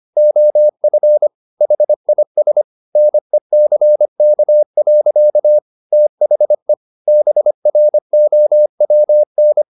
Morse code, heard as a keyed tone, about 25 wpm, 600 Hz. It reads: OF HIS NECK. THE BROWN
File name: the old man and the sea part 1 25wpm